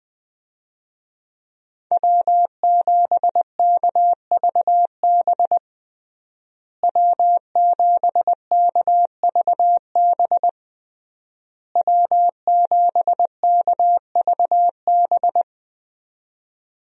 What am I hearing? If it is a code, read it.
Morse code, heard as a keyed tone, about 21 wpm, 700 Hz. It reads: W7KVB W7KVB W7KVB
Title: USA Call Signs - 5 Characters - Repeated 3x - 20wpm